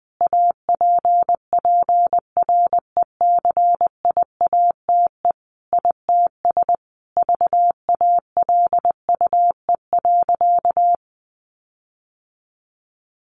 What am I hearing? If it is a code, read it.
Morse code, heard as a keyed tone, about 20 wpm, 700 Hz. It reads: APPRECIATE ITS VALUE.
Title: Sentences from Top 1300 Words 20wpm